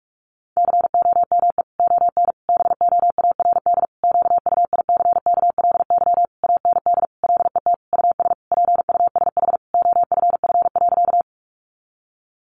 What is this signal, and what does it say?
Morse code, heard as a keyed tone, about 44 wpm, 700 Hz. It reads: COME ON BOARD QUICKLY AND LET US PUSH OFF.
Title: Sentences from Top 1300 Words 45wpm